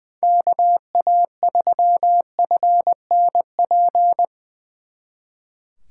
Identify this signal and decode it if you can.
Morse code, heard as a keyed tone, about 23 wpm, 700 Hz. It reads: KA3FNP
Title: Speed Racing - USA 2x3 Call Signs 20wpm